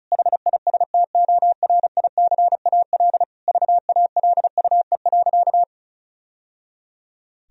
Morse code, heard as a keyed tone, about 34 wpm, 700 Hz. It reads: HISTORICAL VALUE.
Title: Sentences from Top 1000 Words - Review Part 4 - 35wpm